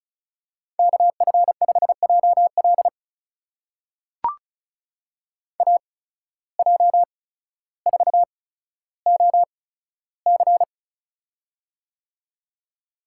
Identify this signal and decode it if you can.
Morse code, heard as a keyed tone, about 36 wpm, spaced as for 15 wpm, 700 Hz. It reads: KF5JL AJ4OC
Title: USA Call Signs - 5 Characters - 8x Character Spacing - 35wpm